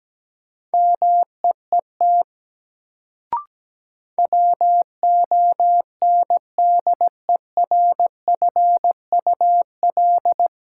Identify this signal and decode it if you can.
Morse code, heard as a keyed tone, about 17 wpm, 700 Hz. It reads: MEET WONDERFUL
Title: Sentences from Top 1100 Words - New Word Drill Alt - 17wpm